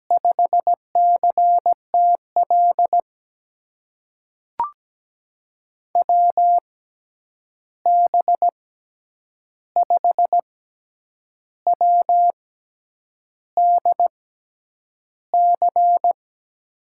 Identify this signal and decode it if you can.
Morse code, heard as a keyed tone, about 17 wpm, spaced as for 8 wpm, 700 Hz. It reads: 5CTL WB5WDC
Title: USA Call Signs - 6 Characters - 6x Character Spacing - 17wpm